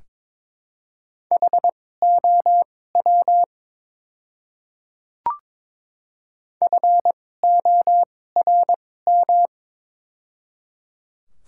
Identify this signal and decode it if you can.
Morse code, heard as a keyed tone, about 22 wpm, spaced as for 16 wpm, 700 Hz. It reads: HOW FORM
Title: 2x Character Spacing - Top 200 Words - 22wpm